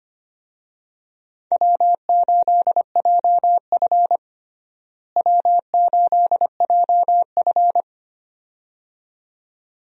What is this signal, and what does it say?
Morse code, heard as a keyed tone, about 25 wpm, 700 Hz. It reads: W8JF W8JF
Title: USA Call Signs - 4 Characters - Repeated 2x - 25wpm